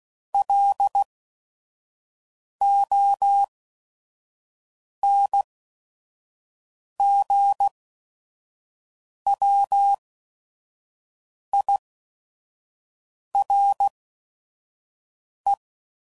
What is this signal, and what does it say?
Morse code, heard as a keyed tone, about 16 wpm, spaced as for 5 wpm, 800 Hz. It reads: LONGWIRE
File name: CW Code Groups @5WPM / fromSM's / Antennas